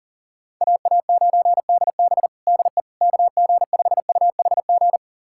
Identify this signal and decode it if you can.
Morse code, heard as a keyed tone, about 37 wpm, 700 Hz. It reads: AA9DB DE KG5UHG
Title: USA Callsign DE Callsign 40wpm